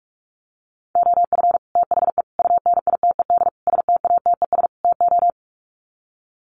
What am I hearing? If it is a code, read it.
Morse code, heard as a keyed tone, about 44 wpm, 700 Hz. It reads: OF THE UNITED STATES TO
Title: Sets of 5 Words Encore 45wpm